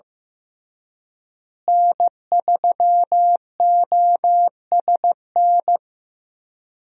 Morse code, heard as a keyed tone, about 15 wpm, 700 Hz. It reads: N3OSN
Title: USA Call Signs - 5 Characters - Repeated 2x - 15wpm